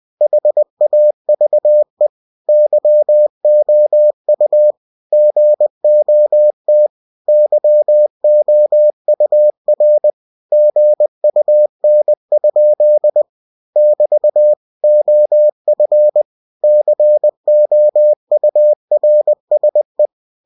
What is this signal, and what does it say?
Morse code, heard as a keyed tone, about 20 wpm, 600 Hz. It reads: HAVE YOU GOT YOUR GUN? <BT> OF COURSE